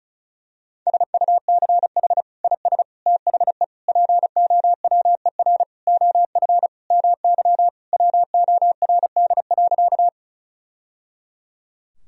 Morse code, heard as a keyed tone, about 36 wpm, 700 Hz. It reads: SUCH IS THE POWER OF MY WORD.
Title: Sentences from Top 1000 Words - Review Part 2 - 35wpm